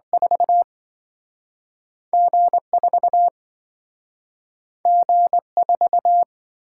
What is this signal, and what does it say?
Morse code, heard as a keyed tone, about 27 wpm, 700 Hz. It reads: 4 G4 G4